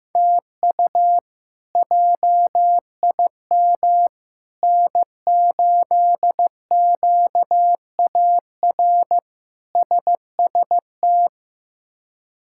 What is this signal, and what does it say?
Morse code, heard as a keyed tone, about 16 wpm, 700 Hz. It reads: TU JIM N8QAR SST
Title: K1USN SST QSO Practice - 15wpm